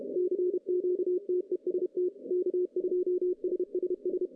Morse code, heard as a keyed tone, about 30 wpm, 370 Hz. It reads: CQTEST K2SSS
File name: 2015 cq ww cw / 2015 cqwww k5tr - 0-41